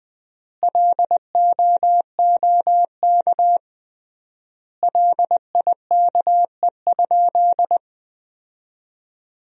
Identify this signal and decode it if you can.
Morse code, heard as a keyed tone, about 20 wpm, 700 Hz. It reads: LOOK LIKE?
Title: 3x Word-Spacing - Sentences from Top 100 Words - 20wpm